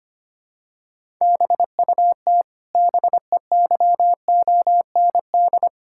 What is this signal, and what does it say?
Morse code, heard as a keyed tone, about 25 wpm, 700 Hz. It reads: BUT BEYOND
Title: Sentences from Top 1000 Words - Review Part 6 - 25wpm